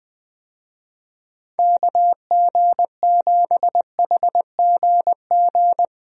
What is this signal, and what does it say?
Morse code, heard as a keyed tone, about 20 wpm, 700 Hz. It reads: KG7HGG
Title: USA Call Signs - 6 Characters - Repeated 3x - 20wpm